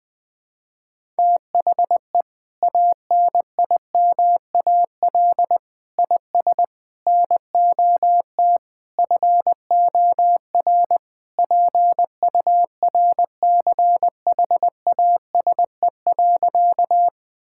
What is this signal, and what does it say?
Morse code, heard as a keyed tone, about 20 wpm, 700 Hz. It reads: THE ANIMAL IS NOT FOR PURCHASE.
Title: Sentences from Top 1000 Words - Review Part 5 - 20wpm